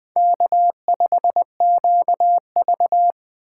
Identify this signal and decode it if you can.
Morse code, heard as a keyed tone, about 21 wpm, 700 Hz. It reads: K5QV
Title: USA Call Signs - 4 Characters - 2x Character Spacing - 20wpm